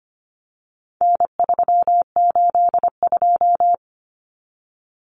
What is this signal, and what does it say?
Morse code, heard as a keyed tone, about 25 wpm, 700 Hz. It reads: N382
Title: Common Cut Numbers - 4-Digit 25wpm